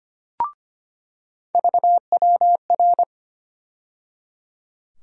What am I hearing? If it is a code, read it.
Morse code, heard as a keyed tone, about 25 wpm, 700 Hz. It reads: VWR
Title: USA Call Sign Postfixes - 3 Letters 25wpm